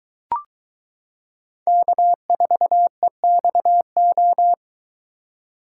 Morse code, heard as a keyed tone, about 22 wpm, 700 Hz. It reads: K4EXO